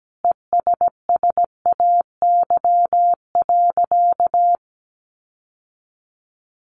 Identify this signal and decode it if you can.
Morse code, heard as a keyed tone, about 17 wpm, 700 Hz. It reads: ESSAY.
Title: Sentences from Top 1500 Words 17wpm